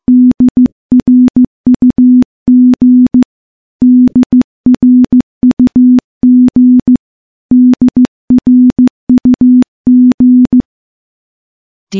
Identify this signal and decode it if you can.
Morse code, heard as a keyed tone, about 14 wpm, 250 Hz. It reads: DRUG DRUG DRUG
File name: morse / morse-training-15WPM